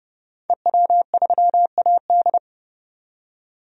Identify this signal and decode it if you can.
Morse code, heard as a keyed tone, about 30 wpm, 700 Hz. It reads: EW3AD
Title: DXCC-100 Call Signs - 5 Characters - 30wpm